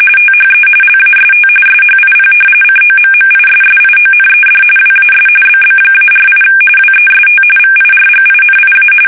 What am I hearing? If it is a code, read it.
RTTY, about 45.45 baud. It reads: THE LAST SHIP COMMISSIONED WAS BB-48 WEST VIRGINIA
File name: fsk850 / battleships